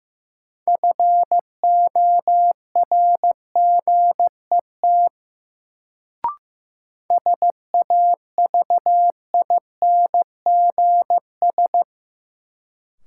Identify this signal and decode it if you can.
Morse code, heard as a keyed tone, about 15 wpm, 700 Hz. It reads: FORGET SAVINGS
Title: Sentences from Top 1200 Words - New Word Drill Alt - 15wpm